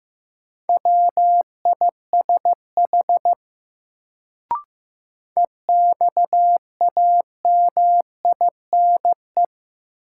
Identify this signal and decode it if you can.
Morse code, heard as a keyed tone, about 15 wpm, 700 Hz. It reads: WISH EXAMINE
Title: Sentences from Top 1600 Words - New Word Drill Alt - 15wpm